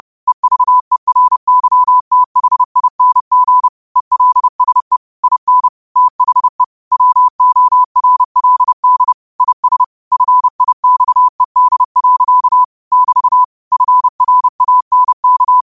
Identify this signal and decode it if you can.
Morse code, heard as a keyed tone, about 30 wpm, 1000 Hz. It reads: EVERYTHING ELSE IN THE WORLD IS FIXED. <BT> FRANK